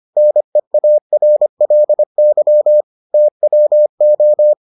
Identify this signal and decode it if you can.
Morse code, heard as a keyed tone, about 25 wpm, 600 Hz. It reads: NEARLY TWO